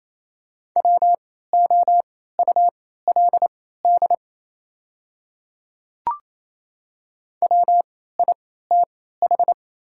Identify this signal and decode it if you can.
Morse code, heard as a keyed tone, about 27 wpm, spaced as for 16 wpm, 700 Hz. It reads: WOULD WITH
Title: 3x Character Spacing - Top 200 Words - 28wpm